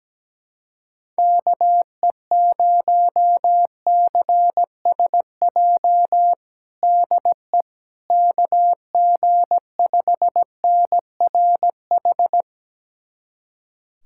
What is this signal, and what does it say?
Morse code, heard as a keyed tone, about 17 wpm, 700 Hz. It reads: KE0CSJ DE KG5NRH
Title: USA Callsign DE Callsign - 6 Characters - 17wpm